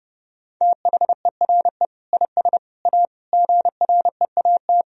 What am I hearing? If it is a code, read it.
Morse code, heard as a keyed tone, about 32 wpm, 700 Hz. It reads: THERE IS A GREAT
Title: Sentences from Top 1000 Words - Review Part 4 - 30wpm